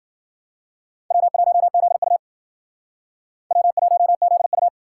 Morse code, heard as a keyed tone, about 53 wpm, 700 Hz. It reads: W1ZU W1ZU